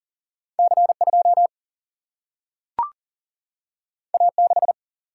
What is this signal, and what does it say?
Morse code, heard as a keyed tone, about 40 wpm, 700 Hz. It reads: C 2 A 6
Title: Single-Letter Single-Number 40wpm